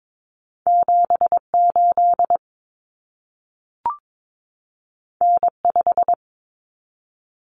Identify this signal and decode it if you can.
Morse code, heard as a keyed tone, about 22 wpm, 700 Hz. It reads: 78 N5
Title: Common Cut Numbers - 2-Digit 22wpm